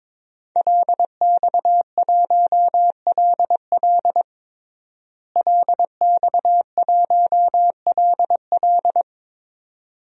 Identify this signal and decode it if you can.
Morse code, heard as a keyed tone, about 22 wpm, 700 Hz. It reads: LX1LL LX1LL
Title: DXCC-100 Call Signs - 5 Characters - Repeated 2x - 22wpm